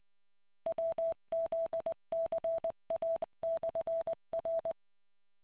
Morse code, heard as a keyed tone, about 20 wpm, 660 Hz. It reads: WZCA/R